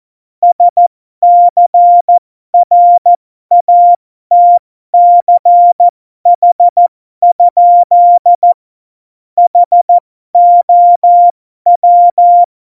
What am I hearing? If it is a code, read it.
Morse code, heard as a keyed tone, about 14 wpm, spaced as for 12 wpm, 700 Hz. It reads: SCRATCH? HOW
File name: Morsekurs / Witze